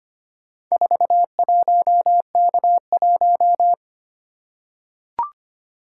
Morse code, heard as a keyed tone, about 25 wpm, 700 Hz. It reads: 41K1
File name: practice / 25wpm / Four Letter-Number Combinations - 25wpm